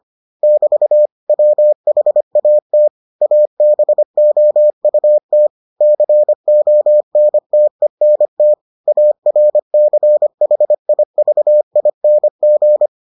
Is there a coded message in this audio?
Yes: Morse code from 0.4 to 12.9 s, heard as a keyed tone, about 25 wpm, 600 Hz. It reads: <BT> WHAT ABOUT CONTENT ARCHIVING